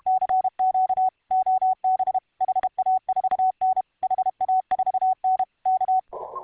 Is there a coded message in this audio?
Yes: Morse code from 0.1 to 6.0 s, heard as a keyed tone, about 32 wpm, 750 Hz. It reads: CQ OB HA4N HA4N K